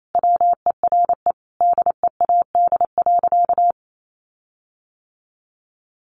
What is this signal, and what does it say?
Morse code, heard as a keyed tone, about 28 wpm, 700 Hz. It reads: WERE DEAD.
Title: Sentences from Top 1500 Words 28wpm